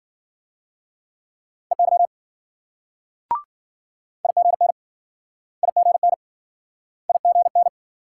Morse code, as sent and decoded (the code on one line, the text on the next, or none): . -.-- / .. --. -. / .. --. -. / .. --. -.
EY IGN IGN IGN